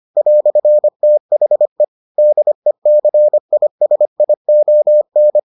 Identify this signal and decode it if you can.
Morse code, heard as a keyed tone, about 25 wpm, 600 Hz. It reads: "THE DECISION